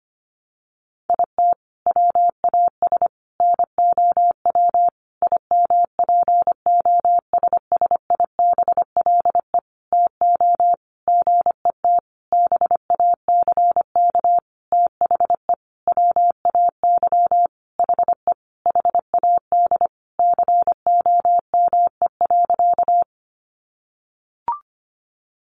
Morse code, heard as a keyed tone, about 25 wpm, 700 Hz. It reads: IT WAS NOW IMPOSSIBLE TO GET BACK THE WAY HE HAD COME.
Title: Sentences from Top 1100 Words 25wpm